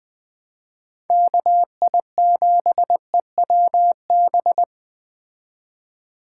Morse code, heard as a keyed tone, about 20 wpm, 700 Hz. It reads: KI7EWB
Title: Speed Racing - USA 2x3 Call Signs 17wpm